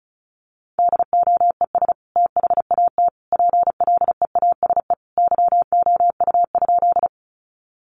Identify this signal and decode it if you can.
Morse code, heard as a keyed tone, about 34 wpm, 700 Hz. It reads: DOES THAT PLEASE YOU?
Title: Sentences from Top 1500 Words 35wpm